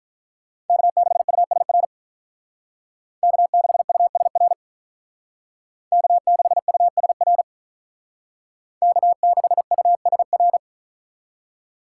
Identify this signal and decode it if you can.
Morse code, heard as a keyed tone, about 45 wpm, 700 Hz. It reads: K6USR K6USR K6USR K6USR